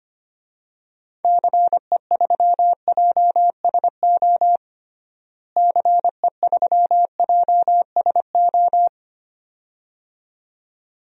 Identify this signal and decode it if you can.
Morse code, heard as a keyed tone, about 25 wpm, 700 Hz. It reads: CE3JSO CE3JSO
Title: DXCC-100 Call Signs - 6 Characters - Repeated 2x - 25wpm